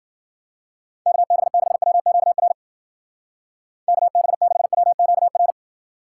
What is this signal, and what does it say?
Morse code, heard as a keyed tone, about 50 wpm, 700 Hz. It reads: KB6WQR KB6WQR